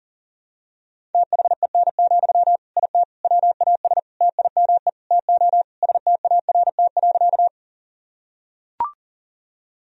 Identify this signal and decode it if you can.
Morse code, heard as a keyed tone, about 40 wpm, 700 Hz. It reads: THEN, IT WAS TIME TO START.
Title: Sentences from Top 1000 Words - Review Part 1 - 40wpm